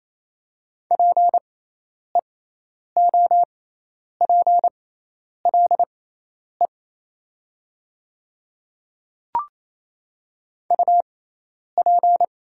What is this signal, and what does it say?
Morse code, heard as a keyed tone, about 27 wpm, spaced as for 10 wpm, 690 Hz. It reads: PEOPLE UP
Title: 6x Character Spacing - Top 100 Words - 28wpm